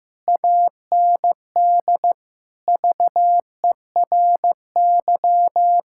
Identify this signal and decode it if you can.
Morse code, heard as a keyed tone, about 15 wpm, 700 Hz. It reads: AND VERY
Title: Sentences from Top 1600 Words 15wpm